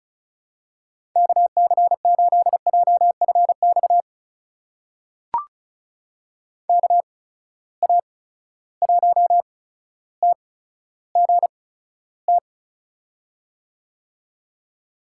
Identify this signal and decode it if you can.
Morse code, heard as a keyed tone, about 36 wpm, spaced as for 15 wpm, 700 Hz. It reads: KC8JFX KA1TGT
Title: USA Call Signs - 6 Characters - 8x Character Spacing - 35wpm